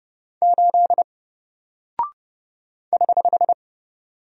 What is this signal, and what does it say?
Morse code, heard as a keyed tone, about 31 wpm, 700 Hz. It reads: 8 <HH>
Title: Single Character (Letter:Number:Punctuation:Prosign) 30wpm